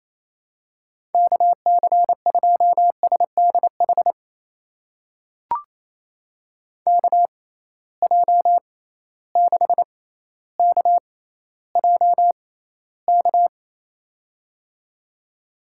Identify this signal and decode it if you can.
Morse code, heard as a keyed tone, about 28 wpm, spaced as for 14 wpm, 700 Hz. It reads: KC2SDH KJ6KJK
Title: USA Call Signs - 6 Characters - 6x Character Spacing - 28wpm